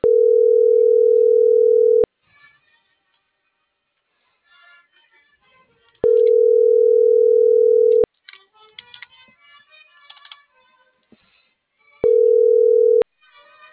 Ringback tone (North America), 440 + 480 Hz.